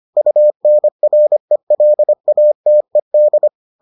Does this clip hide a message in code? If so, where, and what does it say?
Yes: Morse code from 0.2 to 3.5 s, heard as a keyed tone, about 25 wpm, 600 Hz. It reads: UNRELATED